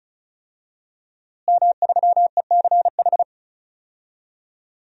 Morse code, heard as a keyed tone, about 36 wpm, 700 Hz. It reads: M3ECH
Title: DXCC-100 Call Signs - 5 Characters - Repeated 2x - 35wpm